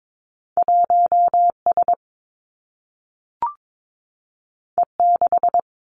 Morse code, heard as a keyed tone, about 22 wpm, 700 Hz. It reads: 1 S E 6
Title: Two Challenging Letter-Number Combinations 22wpm